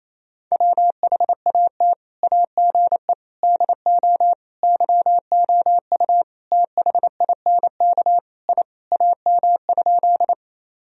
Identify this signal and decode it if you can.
Morse code, heard as a keyed tone, about 28 wpm, 700 Hz. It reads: WHAT AGE DO YOU THINK I AM?
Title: Sentences from Top 1000 Words - Review Part 3 - 28wpm